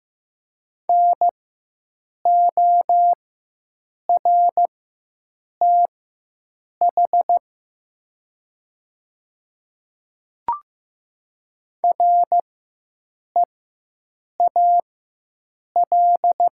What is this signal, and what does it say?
Morse code, heard as a keyed tone, about 15 wpm, spaced as for 7 wpm, 700 Hz. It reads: NORTH REAL
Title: 4x Character Spacing - Top 300 Words - 15wpm